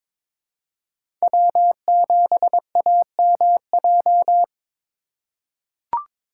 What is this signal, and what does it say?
Morse code, heard as a keyed tone, about 21 wpm, 700 Hz. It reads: W7AMJ
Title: USA Call Signs - 5 Characters - 3x Character Spacing - 22wpm